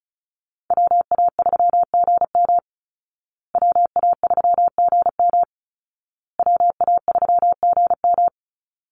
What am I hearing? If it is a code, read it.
Morse code, heard as a keyed tone, about 34 wpm, 700 Hz. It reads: WA3GM WA3GM WA3GM
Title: Call Signs Repeated 2x - 35wpm